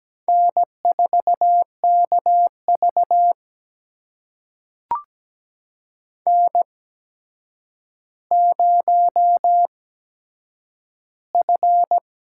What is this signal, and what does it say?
Morse code, heard as a keyed tone, about 17 wpm, spaced as for 8 wpm, 700 Hz. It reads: N4KV N0F
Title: USA Call Signs - 4 Characters - 8x Character Spacing - 17wpm